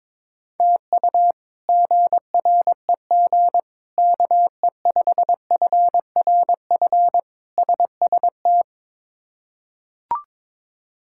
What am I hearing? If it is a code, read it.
Morse code, heard as a keyed tone, about 22 wpm, 700 Hz. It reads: TU GREG KE5FRF SST
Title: K1USN SST QSO Practice - 22wpm